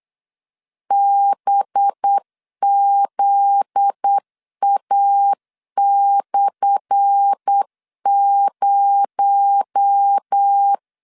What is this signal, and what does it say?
Morse code, heard as a keyed tone, about 8 wpm, 800 Hz. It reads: BZA/0
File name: LEZIONI / 10 SIGNS / 10A SLASH 8wpm 3452